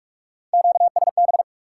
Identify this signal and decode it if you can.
Morse code, heard as a keyed tone, about 60 wpm, spaced as for 52 wpm, 700 Hz. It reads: QSB
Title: Speed-Racing Top 90 QSO Elements 45wpm